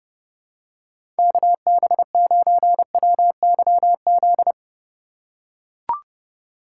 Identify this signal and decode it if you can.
Morse code, heard as a keyed tone, about 30 wpm, 700 Hz. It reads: KB9WYZ
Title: USA Call Signs - 6 Characters - 2x Character Spacing - 30wpm